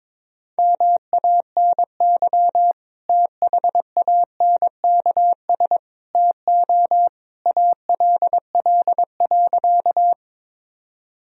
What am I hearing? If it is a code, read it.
Morse code, heard as a keyed tone, about 22 wpm, 700 Hz. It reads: MANY THANKS TO ALL.
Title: Sentences from Top 1000 Words - Review Part 3 - 22wpm